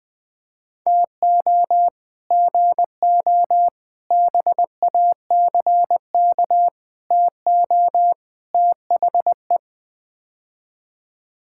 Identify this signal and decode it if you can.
Morse code, heard as a keyed tone, about 20 wpm, 700 Hz. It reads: TO GO BACK TO THE
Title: Build A Phrase - 5-Word Phrases - 20wpm